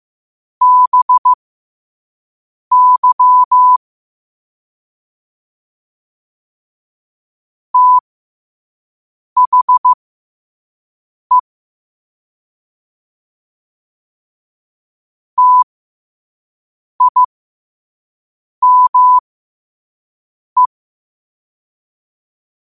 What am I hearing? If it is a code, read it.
Morse code, heard as a keyed tone, about 15 wpm, spaced as for 5 wpm, 1000 Hz. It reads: BY THE TIME